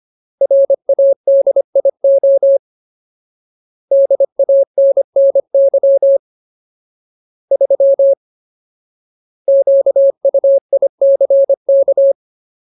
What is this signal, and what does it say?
Morse code, heard as a keyed tone, about 25 wpm, 550 Hz. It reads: RADIO DANNY 3 QUICK
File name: Ultimate Word List / 25WPM Ultimate